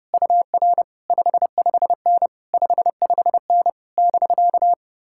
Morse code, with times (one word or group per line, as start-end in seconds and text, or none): 0.1-0.8 UR
1.1-2.3 55N
2.5-3.7 55N
4.0-4.7 <BK>